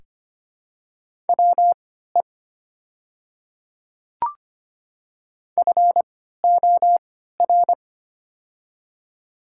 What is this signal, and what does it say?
Morse code, heard as a keyed tone, about 25 wpm, spaced as for 14 wpm, 700 Hz. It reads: WE FOR
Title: 3x Character Spacing - Top 200 Words - 25wpm